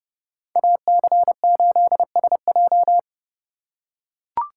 Morse code, heard as a keyed tone, about 31 wpm, 700 Hz. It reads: AC8SJ